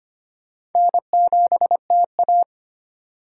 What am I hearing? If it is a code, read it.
Morse code, heard as a keyed tone, about 25 wpm, 700 Hz. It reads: N7TA
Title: USA Call Signs - 4 Characters - 4x Character Spacing - 25wpm